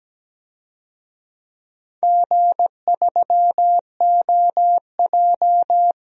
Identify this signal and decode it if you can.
Morse code, heard as a keyed tone, about 17 wpm, 700 Hz. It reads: G3OJ
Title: DXCC-100 Call Signs - 4 Characters - 17wpm